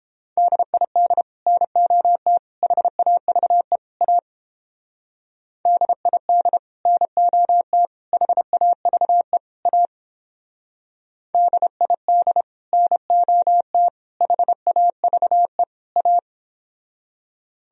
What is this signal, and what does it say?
Morse code, heard as a keyed tone, about 29 wpm, 700 Hz. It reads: DID NOT HAVE A DID NOT HAVE A DID NOT HAVE A